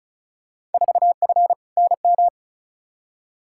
Morse code, heard as a keyed tone, about 36 wpm, 700 Hz. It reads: 4F NM
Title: Field Day Exchanges Encore - 35wpm